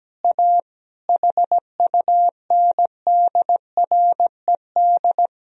Morse code, heard as a keyed tone, about 17 wpm, 700 Hz. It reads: A HUNDRED
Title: Sentences from Top 1000 Words - Review Part 3 - 17wpm